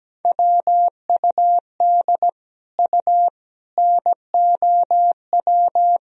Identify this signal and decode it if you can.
Morse code, heard as a keyed tone, about 17 wpm, 700 Hz. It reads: WUD U NOW
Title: Sentences from Top 200 Words and Abbreviations 17wpm